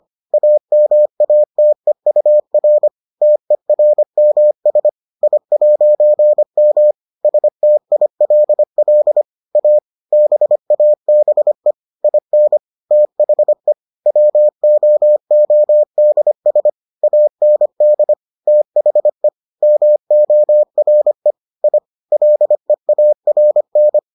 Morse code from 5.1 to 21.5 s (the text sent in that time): I'M STILL A BABE IN THE WOODS AND THE MORE